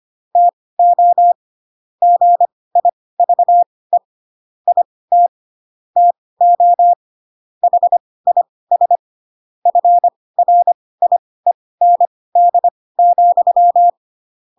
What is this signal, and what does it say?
Morse code, heard as a keyed tone, about 25 wpm, spaced as for 18 wpm, 700 Hz. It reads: TO GIVE IT TO HIS FRIEND,
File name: Pinocchio / Pinocchio ch2 12 25 ik1hge